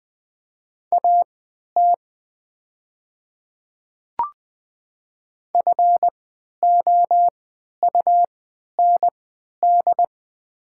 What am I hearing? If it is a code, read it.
Morse code, heard as a keyed tone, about 20 wpm, spaced as for 11 wpm, 700 Hz. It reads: AT FOUND
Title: 3x Character Spacing - Top 300 Words - 20wpm